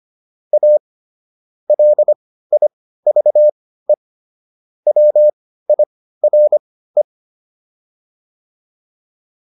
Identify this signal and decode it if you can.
Morse code, heard as a keyed tone, about 25 wpm, spaced as for 15 wpm, 600 Hz. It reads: A LIVE WIRE